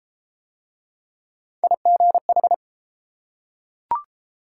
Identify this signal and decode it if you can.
Morse code, heard as a keyed tone, about 32 wpm, 700 Hz. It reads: IGH